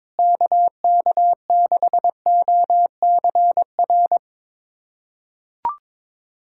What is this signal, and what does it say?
Morse code, heard as a keyed tone, about 22 wpm, 700 Hz. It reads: KK6OCR
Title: USA Callsign DE Callsign 22wpm